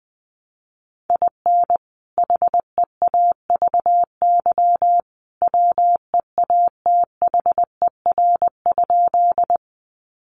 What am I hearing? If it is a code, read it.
Morse code, heard as a keyed tone, about 20 wpm, 700 Hz. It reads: IN HEAVY WEATHER?
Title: Sentences from Top 1100 Words 20wpm